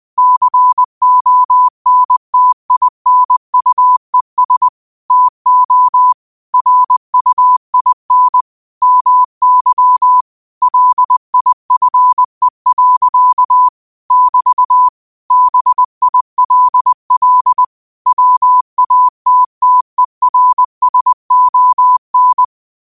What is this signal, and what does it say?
Morse code, heard as a keyed tone, about 20 wpm, 1000 Hz. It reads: CONTINUES TO RUIN MY LIFE. <BT> BILL WATTERSON
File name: qotdfiles / 20wpm